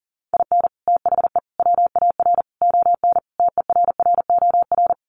Morse code, heard as a keyed tone, about 40 wpm, 700 Hz. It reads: IN THE WAR ON TERROR
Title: Sets of 5 Words Encore2 40wpm